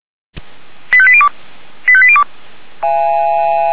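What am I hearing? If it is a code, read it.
Five-tone selective call (ZVEI), heard as quick runs of five tones, tones of about 70 ms. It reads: 85802 85802
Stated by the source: Five-tone ZVEI selective call received on an SDR.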